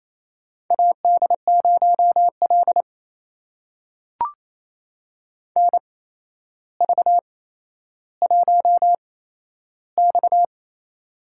Morse code, heard as a keyed tone, about 28 wpm, spaced as for 12 wpm, 700 Hz. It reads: AD0L NV1X
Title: USA Call Signs - 4 Characters - 8x Character Spacing - 28wpm